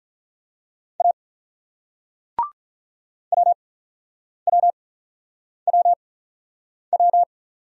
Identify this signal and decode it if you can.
Morse code, heard as a keyed tone, about 47 wpm, 700 Hz. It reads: A W W W W